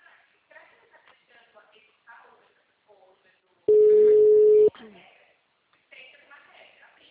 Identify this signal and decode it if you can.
Ringback tone (Europe, 425 Hz), with speech.